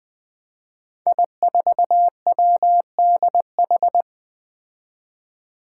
Morse code, heard as a keyed tone, about 20 wpm, 700 Hz. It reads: I4WDH
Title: Five Challenging Letter-Number Combinations 20wpm